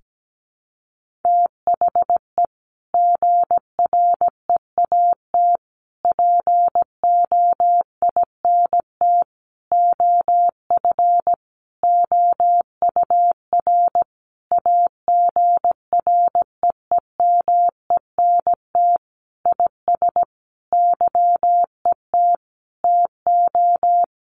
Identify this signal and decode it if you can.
Morse code, heard as a keyed tone, about 17 wpm, 700 Hz. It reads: THE GREAT POINT OF OUR AGREEMENT IS YET TO
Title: Sentences from Top 1500 Words 17wpm